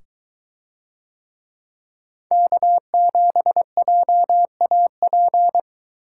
Morse code, heard as a keyed tone, about 22 wpm, 700 Hz. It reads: K7JAP